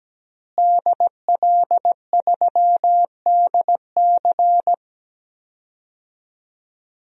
Morse code, heard as a keyed tone, about 17 wpm, 700 Hz. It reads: DL3DC
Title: DXCC-100 Call Signs - 5 Characters - Repeated 3x - 17wpm